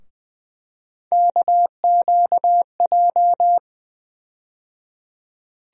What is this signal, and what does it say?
Morse code, heard as a keyed tone, about 20 wpm, 700 Hz. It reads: KQJ
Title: USA Call Sign Postfixes - 3 Letters 20wpm